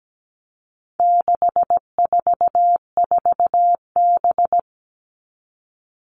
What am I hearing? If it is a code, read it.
Morse code, heard as a keyed tone, about 17 wpm, 700 Hz. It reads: 644B
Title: All Cut Numbers - 4-Digit 17wpm